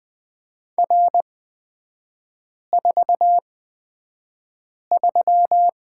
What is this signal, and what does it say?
Morse code, heard as a keyed tone, about 20 wpm, spaced as for 5 wpm, 700 Hz. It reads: R43
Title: 255 Lesson 41e - Period - Three Characters 20wpm